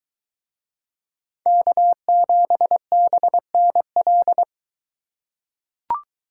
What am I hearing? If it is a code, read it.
Morse code, heard as a keyed tone, about 22 wpm, 700 Hz. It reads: K7BNL